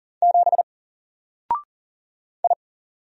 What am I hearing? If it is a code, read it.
Morse code, heard as a keyed tone, about 40 wpm, 700 Hz. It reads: 7 I
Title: Single Letter-Number Alt 40wpm